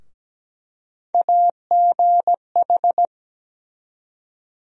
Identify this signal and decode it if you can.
Morse code, heard as a keyed tone, about 17 wpm, 700 Hz. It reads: AGH
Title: USA Call Sign Postfixes - 3 Letters 17wpm